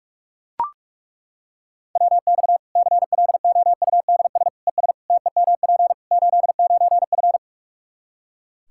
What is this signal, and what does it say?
Morse code, heard as a keyed tone, about 41 wpm, 700 Hz. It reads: WX CLOUDS ES TEMP 89F
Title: Practice QSO - Exchange 2 - Info - 45wpm